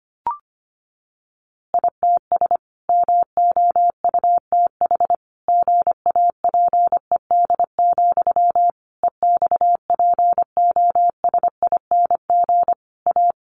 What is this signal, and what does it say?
Morse code, heard as a keyed tone, about 25 wpm, 700 Hz. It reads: ITS MOUTH GAPED, EXPOSING A
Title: A Sound Of Thunder 25wpm